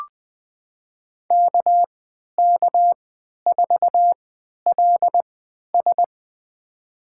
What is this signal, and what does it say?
Morse code, heard as a keyed tone, about 20 wpm, spaced as for 11 wpm, 700 Hz. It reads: KK4LS
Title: USA Call Signs - 5 Characters - 3x Character Spacing - 20wpm